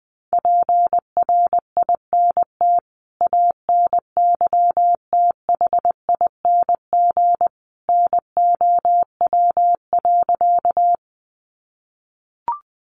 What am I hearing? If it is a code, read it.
Morse code, heard as a keyed tone, about 20 wpm, 700 Hz. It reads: PRINT ANYTHING NOW.
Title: Sentences from Top 1400 Words 20wpm